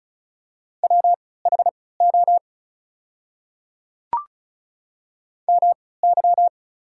Morse code, heard as a keyed tone, about 36 wpm, spaced as for 20 wpm, 690 Hz. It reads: WHO MY
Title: 3x Character Spacing - Top 100 Words - 35wpm